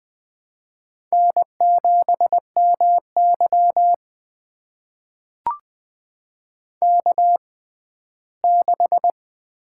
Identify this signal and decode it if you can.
Morse code, heard as a keyed tone, about 20 wpm, spaced as for 14 wpm, 700 Hz. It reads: N7MY K6
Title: USA Call Signs - 4 Characters - 6x Character Spacing - 20wpm